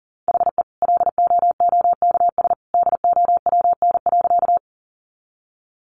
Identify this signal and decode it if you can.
Morse code, heard as a keyed tone, about 40 wpm, 700 Hz. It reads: HE LOOKS DOWN.
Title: Additional Easy Phrases 40wpm